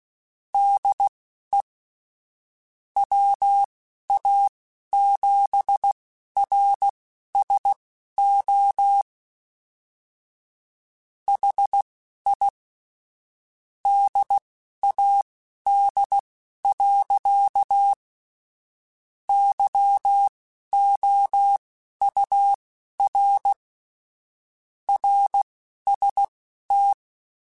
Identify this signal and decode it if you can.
Morse code, heard as a keyed tone, about 16 wpm, spaced as for 11 wpm, 800 Hz. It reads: DE WA7RSO HI DAD. YOUR RST
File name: QSOs / QSOs-mp3s / WD0BGP 2@10